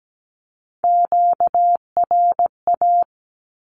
Morse code, heard as a keyed tone, about 17 wpm, 700 Hz. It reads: QRA
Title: Q-Codes 17wpm